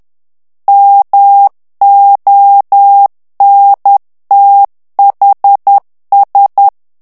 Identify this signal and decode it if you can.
Morse code, heard as a keyed tone, about 10 wpm, 800 Hz. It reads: MONTHS
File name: ANGLAIS texte clair 3 vitesse 10 mots minute